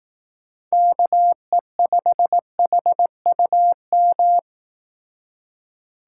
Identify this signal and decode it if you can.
Morse code, heard as a keyed tone, about 18 wpm, 700 Hz. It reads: KE5HUM
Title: Speed Racing - USA 2x3 Call Signs 15wpm